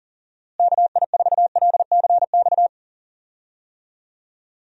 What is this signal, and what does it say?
Morse code, heard as a keyed tone, about 37 wpm, 700 Hz. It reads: KI4LCX
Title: USA Call Signs - 6 Characters - Repeated 2x - 40wpm